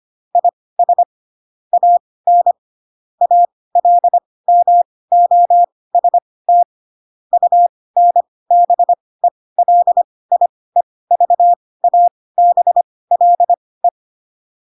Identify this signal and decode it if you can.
Morse code, heard as a keyed tone, about 25 wpm, spaced as for 18 wpm, 700 Hz. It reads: IS AN ALMOST UNBELIEVABLE
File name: Pinocchio ch3 12 25 ik1hge